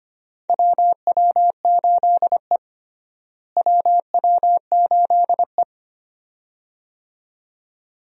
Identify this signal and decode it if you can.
Morse code, heard as a keyed tone, about 25 wpm, 700 Hz. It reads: WW8E WW8E
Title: USA Call Signs - 4 Characters - Repeated 2x - 25wpm